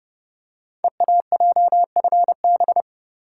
Morse code, heard as a keyed tone, about 29 wpm, 700 Hz. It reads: EAJFB
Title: Code Groups - 5 Letter:Numbers - 30wpm